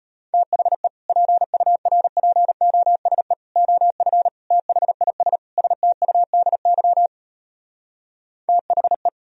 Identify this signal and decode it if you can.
Morse code, heard as a keyed tone, about 33 wpm, 700 Hz. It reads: THE PURPOSE OF THIS STUDY THE